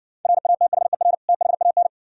Morse code, heard as a keyed tone, about 68 wpm, 700 Hz. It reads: RATHER THAN